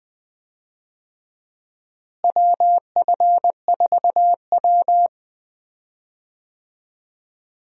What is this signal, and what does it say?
Morse code, heard as a keyed tone, about 20 wpm, 700 Hz. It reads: WF4W
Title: USA Call Signs - 4 Characters - Repeated 2x - 20wpm